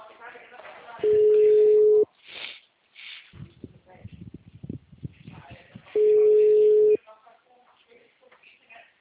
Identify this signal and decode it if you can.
Ringback tone (Europe, 425 Hz), with speech.
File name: out- -2016- - - .4370